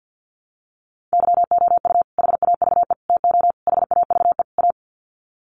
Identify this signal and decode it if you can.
Morse code, heard as a keyed tone, about 50 wpm, 700 Hz. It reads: YOU HAVE TO HAVE A
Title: Sets of 5 Words Encore 50wpm